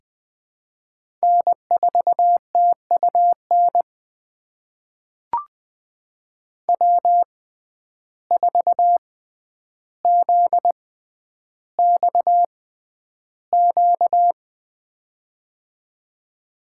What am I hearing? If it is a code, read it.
Morse code, heard as a keyed tone, about 20 wpm, spaced as for 10 wpm, 700 Hz. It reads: N4TUN W4ZXQ
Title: USA Call Signs - 6x Character Spacing - 20wpm